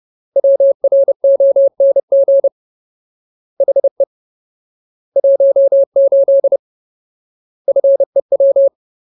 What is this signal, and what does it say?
Morse code, heard as a keyed tone, about 30 wpm, 550 Hz. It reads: WRONG HE 18 FEW
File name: Ultimate Word List / 30WPM Ultimate